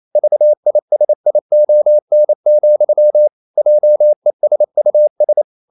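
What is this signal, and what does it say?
Morse code, heard as a keyed tone, about 28 wpm, 600 Hz. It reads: VISION, JESUS
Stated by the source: Morse code practice at 28 wpm, text of Ken Follett's novel Never in English, from CW med Gnister.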